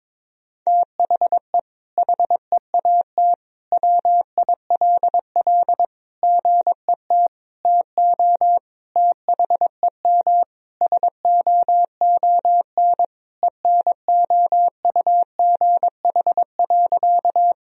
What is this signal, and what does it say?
Morse code, heard as a keyed tone, about 22 wpm, 700 Hz. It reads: THE HEAT WILL GET TO THEM SOON ENOUGH.
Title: Sentences from Top 1000 Words - Review Part 3 - 22wpm